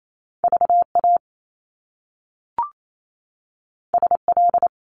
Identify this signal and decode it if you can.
Morse code, heard as a keyed tone, about 28 wpm, 700 Hz. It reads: VA SL
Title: Top Two Letter Combinations in Words 28wpm